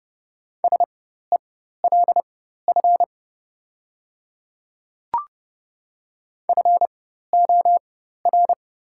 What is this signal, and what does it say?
Morse code, heard as a keyed tone, about 31 wpm, spaced as for 14 wpm, 700 Hz. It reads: SELF FOR
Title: 4x Character Spacing - Top 200 Words - 30wpm